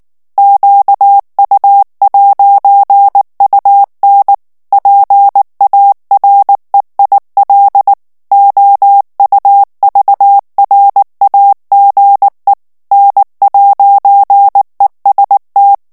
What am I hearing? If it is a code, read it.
Morse code, heard as a keyed tone, about 19 wpm, 800 Hz. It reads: QU'UN PAREIL OUVRAGE N'EST
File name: FRANCAIS TEXTE CLAIR 2 VITESSE 18 MOTS MINUTES